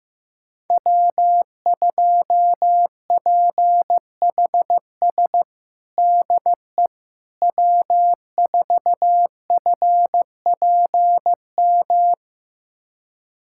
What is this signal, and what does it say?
Morse code, heard as a keyed tone, about 15 wpm, 700 Hz. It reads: W2PHS DE W4FPM
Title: USA Callsign DE Callsign - 5 Characters - 15wpm